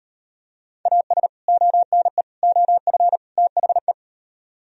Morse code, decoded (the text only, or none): AS ONE OF THE